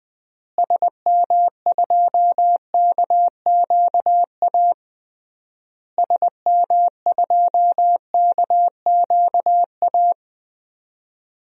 Morse code, heard as a keyed tone, about 20 wpm, 700 Hz. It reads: SM2KQA SM2KQA
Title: DXCC-100 Call Signs - 6 Characters - Repeated 3x - 20wpm